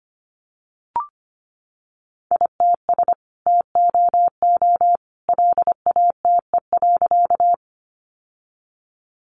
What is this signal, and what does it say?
Morse code, heard as a keyed tone, about 25 wpm, 700 Hz. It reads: ITS TOO LATE.
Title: Sentences from Top 600 Words 25wpm